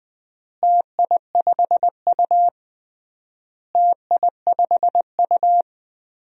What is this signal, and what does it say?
Morse code, heard as a keyed tone, about 21 wpm, 700 Hz. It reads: TI5U TI5U
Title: DXCC-100 Call Signs - 4 Characters - Repeated 3x - 20wpm